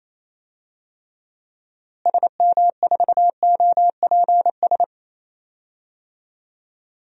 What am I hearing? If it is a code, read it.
Morse code, heard as a keyed tone, about 28 wpm, 700 Hz. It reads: SM4OPS
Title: DXCC-100 Call Signs - 6 Characters - Repeated 3x - 28wpm